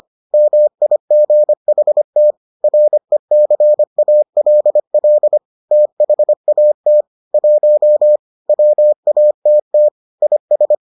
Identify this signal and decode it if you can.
Morse code, heard as a keyed tone, about 25 wpm, 600 Hz. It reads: MIGHT RECALL THAT 1 WATT IS